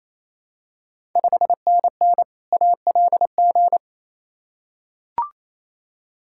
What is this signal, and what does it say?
Morse code, heard as a keyed tone, about 28 wpm, 700 Hz. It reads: 5NN ALG
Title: Ontario QSO Party Exchange 28wpm